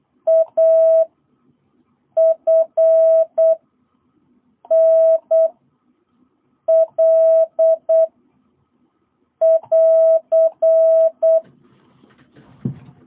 Morse code, heard as a keyed tone, about 8 wpm, 650 Hz. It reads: A F N L <AR>
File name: jh8cbh / 2022cw